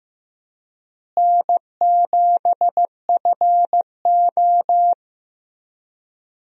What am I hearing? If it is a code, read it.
Morse code, heard as a keyed tone, about 15 wpm, 700 Hz. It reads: N7FO
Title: USA Call Signs - 4 Characters - Repeated 3x - 15wpm